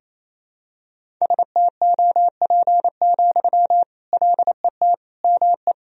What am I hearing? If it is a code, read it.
Morse code, heard as a keyed tone, about 28 wpm, 700 Hz. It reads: STOP, LET ME
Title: Sentences from Top 300 Words and Abbreviations 28wpm